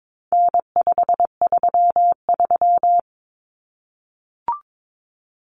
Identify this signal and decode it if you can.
Morse code, heard as a keyed tone, about 22 wpm, 700 Hz. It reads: N533
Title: Common Cut Numbers - 4-Digit 22wpm